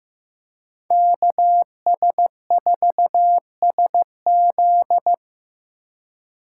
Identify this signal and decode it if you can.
Morse code, heard as a keyed tone, about 18 wpm, 700 Hz. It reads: KS4SZ
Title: Speed Racing - USA 2x2 Call Signs 15wpm